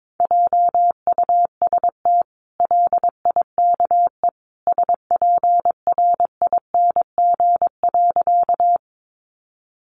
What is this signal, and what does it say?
Morse code, heard as a keyed tone, about 22 wpm, 700 Hz. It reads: JUST LIKE SPRING.
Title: Sentences from Top 1200 Words 22wpm